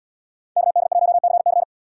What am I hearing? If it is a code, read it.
Morse code, heard as a keyed tone, about 71 wpm, 700 Hz. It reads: KD1CC